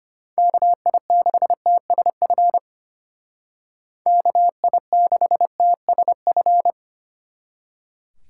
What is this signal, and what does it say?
Morse code, heard as a keyed tone, about 31 wpm, 700 Hz. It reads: KI6TSF KI6TSF